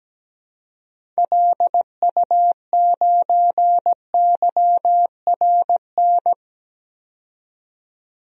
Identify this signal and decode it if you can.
Morse code, heard as a keyed tone, about 17 wpm, 700 Hz. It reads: LU9YRN
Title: DXCC-100 Call Signs - 6 Characters - 17wpm